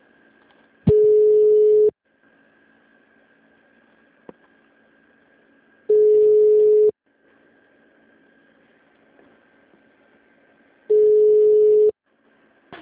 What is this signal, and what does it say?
Ringback tone (Europe, 425 Hz).
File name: out- -2006- - - .128